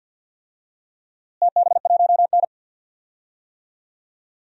Morse code, heard as a keyed tone, about 52 wpm, 700 Hz. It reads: TB1N
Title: DXCC-100 Call Signs - 4 Characters - Repeated 2x - 50wpm